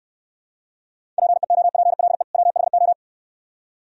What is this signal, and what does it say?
Morse code, heard as a keyed tone, about 64 wpm, 700 Hz. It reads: PEOPLE WHO